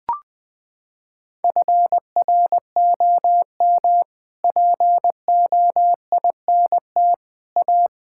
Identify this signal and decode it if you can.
Morse code, heard as a keyed tone, about 20 wpm, 700 Hz. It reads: FROM POINT A
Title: Sets of 6 Words - 20wpm